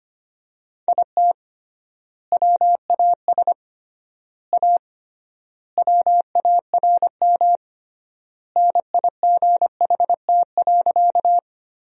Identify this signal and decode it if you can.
Morse code, heard as a keyed tone, about 25 wpm, 700 Hz. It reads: IT WAS A WARM NIGHT.
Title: 3x Word-Spacing - Sentences from Top 500 Words - 25wpm